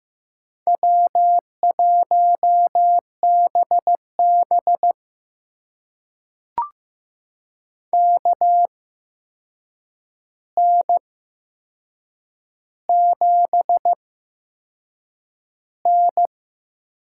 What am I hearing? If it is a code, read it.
Morse code, heard as a keyed tone, about 15 wpm, spaced as for 6 wpm, 700 Hz. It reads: W1BB KN7N
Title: USA Call Signs - 4 Characters - 8x Character Spacing - 15wpm